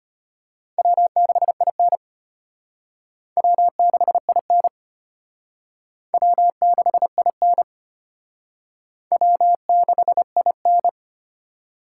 Morse code, heard as a keyed tone, about 31 wpm, 700 Hz. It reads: W6IN W6IN W6IN W6IN
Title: Speed Racing - USA 1x2 Call Signs 25wpm